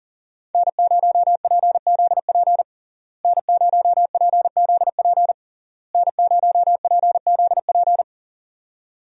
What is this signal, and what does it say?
Morse code, heard as a keyed tone, about 40 wpm, 700 Hz. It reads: N0PZP N0PZP N0PZP
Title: USA Call Signs - 5 Characters - Repeated 3x - 40wpm